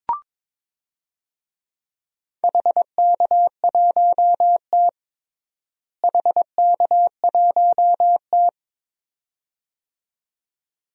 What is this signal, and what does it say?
Morse code, heard as a keyed tone, about 22 wpm, 700 Hz. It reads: HK1T HK1T
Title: DXCC-100 Call Signs - 4 Characters - Repeated 2x - 22wpm